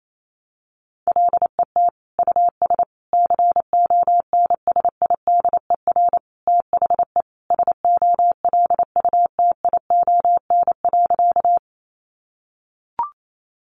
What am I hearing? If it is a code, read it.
Morse code, heard as a keyed tone, about 28 wpm, 700 Hz. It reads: LET US CONSIDER THE SOLUTION.
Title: Sentences from Top 1300 Words 28wpm